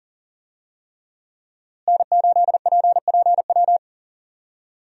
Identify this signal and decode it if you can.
Morse code, heard as a keyed tone, about 40 wpm, 700 Hz. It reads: N8PPW
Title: USA Call Signs - 5 Characters - Repeated 3x - 40wpm